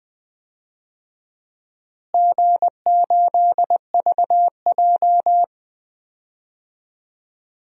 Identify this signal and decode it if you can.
Morse code, heard as a keyed tone, about 20 wpm, 700 Hz. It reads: G8VJ
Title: DXCC-100 Call Signs - 4 Characters - Repeated 3x - 20wpm